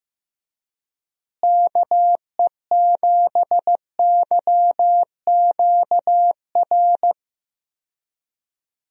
Morse code, heard as a keyed tone, about 16 wpm, 700 Hz. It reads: KE7YQR
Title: USA Call Signs - 6 Characters - Repeated 2x - 15wpm